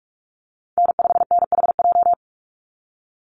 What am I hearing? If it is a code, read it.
Morse code, heard as a keyed tone, about 44 wpm, 700 Hz. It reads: N5NHJ
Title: Call Signs - 45wpm